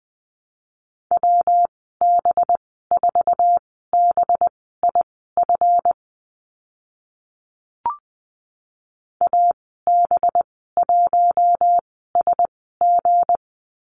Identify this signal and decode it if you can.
Morse code, heard as a keyed tone, about 20 wpm, spaced as for 14 wpm, 700 Hz. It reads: WB4BIF AB1SG
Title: 231 Lesson 37g - Letter x - Call Signs 20wpm